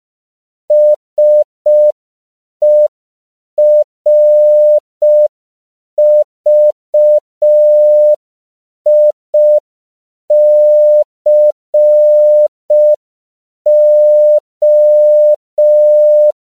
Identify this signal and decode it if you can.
Morse code, heard as a keyed tone, about 5 wpm, 600 Hz. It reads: SERVICO
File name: CW / 5PPM